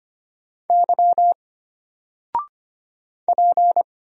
Morse code, heard as a keyed tone, about 25 wpm, 700 Hz. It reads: Y P
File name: Single-Letter-Number-25wpm